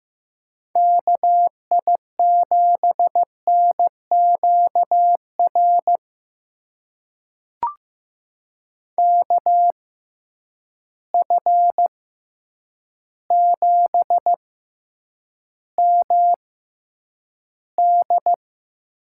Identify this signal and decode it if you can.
Morse code, heard as a keyed tone, about 15 wpm, spaced as for 8 wpm, 700 Hz. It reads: KI7NQR KF7MD
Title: USA Call Signs - 6 Characters - 6x Character Spacing - 15wpm